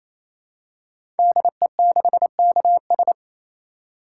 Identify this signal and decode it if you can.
Morse code, heard as a keyed tone, about 27 wpm, 700 Hz. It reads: DE6KS
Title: Code Groups - 5 Letter:Numbers - 28wpm